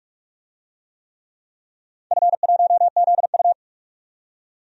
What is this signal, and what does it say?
Morse code, heard as a keyed tone, about 44 wpm, 700 Hz. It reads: F1ZU
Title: DXCC-100 Call Signs - 4 Characters - 45wpm